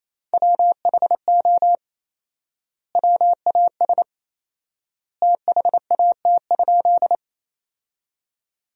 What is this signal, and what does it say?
Morse code, heard as a keyed tone, about 28 wpm, 700 Hz. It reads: WHO WAS THAT?
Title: 4x Word-Spacing - Sentences from Top 100 Words - 28wpm